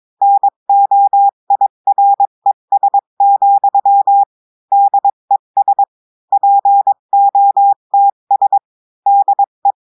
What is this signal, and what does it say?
Morse code, heard as a keyed tone, about 23 wpm, 800 Hz. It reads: NOIRES, DES POTS DE
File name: cours-du-mercredi / audios / cours-853-du-2-mai-2024-etxte chiffre et texte clair-a-20-wpm